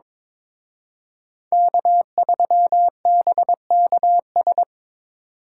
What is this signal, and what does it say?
Morse code, heard as a keyed tone, about 26 wpm, 700 Hz. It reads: K3BKS
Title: Speed Racing - USA 1x3 Call Signs 22wpm